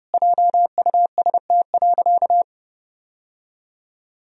Morse code, heard as a keyed tone, about 30 wpm, 700 Hz. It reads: JUST.
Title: Sentences from Top 1000 Words - Review Part 1 - 30wpm